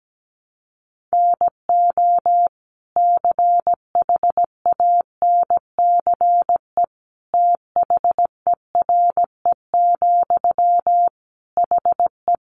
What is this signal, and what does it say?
Morse code, heard as a keyed tone, about 17 wpm, 700 Hz. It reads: NO CHANCE THERE, HE
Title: Sentences from Top 800 Words 17wpm